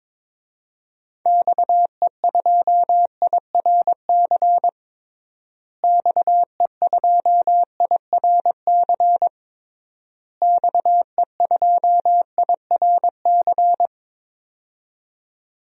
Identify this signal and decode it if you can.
Morse code, heard as a keyed tone, about 22 wpm, 700 Hz. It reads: XE2IRC XE2IRC XE2IRC
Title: DXCC-100 Call Signs - 6 Characters - Repeated 3x - 22wpm